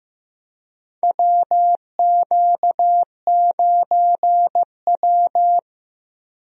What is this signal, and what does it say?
Morse code, heard as a keyed tone, about 16 wpm, 700 Hz. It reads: WQ9W
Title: USA Call Signs - 4 Characters - Repeated 2x - 15wpm